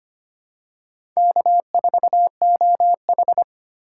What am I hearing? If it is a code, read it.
Morse code, heard as a keyed tone, about 25 wpm, 700 Hz. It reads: K4OH
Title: USA Call Signs - 4 Characters - 8x Character Spacing - 25wpm